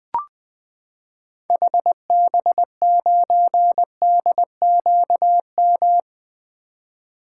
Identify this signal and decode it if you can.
Morse code, heard as a keyed tone, about 20 wpm, 700 Hz. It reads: HB9DQM
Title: 2023 SOTA Activator Call Signs - 20wpm